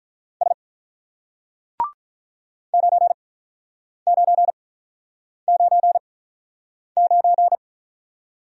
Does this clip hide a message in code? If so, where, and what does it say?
Yes: Morse code from 0.4 to 7.6 s, heard as a keyed tone, about 48 wpm, spaced as for 9 wpm, 690 Hz. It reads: S 9999